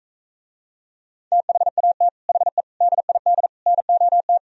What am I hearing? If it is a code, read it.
Morse code, heard as a keyed tone, about 45 wpm, 700 Hz. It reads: THAT HE DID NOT